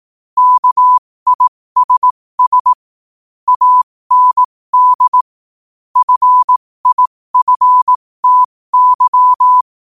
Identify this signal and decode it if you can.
Morse code, heard as a keyed tone, about 18 wpm, spaced as for 15 wpm, 1000 Hz. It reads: KISS AND FIFTY
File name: 15wpm